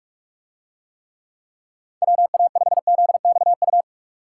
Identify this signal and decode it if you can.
Morse code, heard as a keyed tone, about 45 wpm, 700 Hz. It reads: WA5ZXU